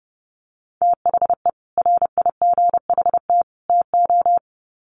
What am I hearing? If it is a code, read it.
Morse code, heard as a keyed tone, about 30 wpm, 700 Hz. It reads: THE RIGHT TO
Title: Sets of 3 Words 30wpm